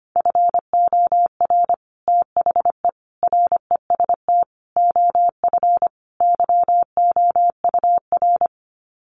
Morse code, heard as a keyed tone, about 25 wpm, 700 Hz. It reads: FOR THE REST OF YOUR
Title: Sets of 5 Words Encore 25wpm